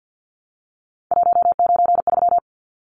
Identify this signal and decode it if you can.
Morse code, heard as a keyed tone, about 50 wpm, 700 Hz. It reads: 193
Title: Numbers - 3-Digit 50wpm